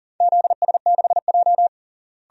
Morse code, heard as a keyed tone, about 40 wpm, 700 Hz. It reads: ZS6J
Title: DXCC-100 Call Signs - 4 Characters - Repeated 3x - 40wpm